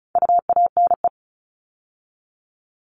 Morse code, heard as a keyed tone, about 34 wpm, 700 Hz. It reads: UANE
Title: All Cut Numbers - 4-Digit 35wpm